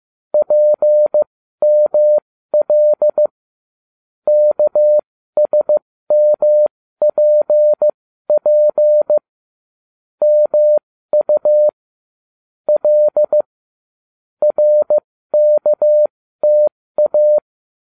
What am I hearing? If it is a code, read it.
Morse code, heard as a keyed tone, about 15 wpm, spaced as for 13 wpm, 600 Hz. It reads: PML KSMPP MU L RKTA